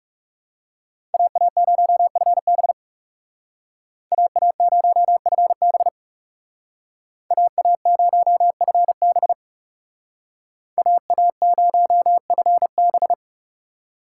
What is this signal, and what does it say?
Morse code, heard as a keyed tone, about 38 wpm, 700 Hz. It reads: AA0FB AA0FB AA0FB AA0FB